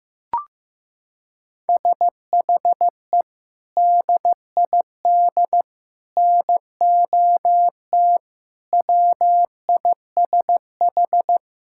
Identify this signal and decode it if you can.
Morse code, heard as a keyed tone, about 15 wpm, 700 Hz. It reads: SHE DID NOT WISH
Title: Sentences from Top 1600 Words 15wpm